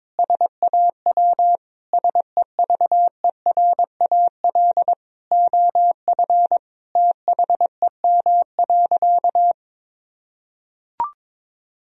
Morse code, heard as a keyed tone, about 22 wpm, 700 Hz. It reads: SAW SEVERAL OF THEM.
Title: Sentences from Top 1000 Words - Review Part 2 - 22wpm